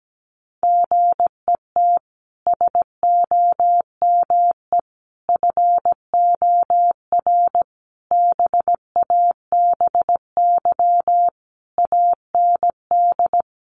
Morse code, heard as a keyed tone, about 17 wpm, 700 Hz. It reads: GET SOME FOR BABY AND
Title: Sentences from Top 1300 Words 17wpm